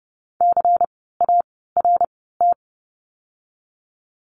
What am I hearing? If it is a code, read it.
Morse code, heard as a keyed tone, about 30 wpm, 700 Hz. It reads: C A R T
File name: Lesson 16f - Letter c - Words 30wpm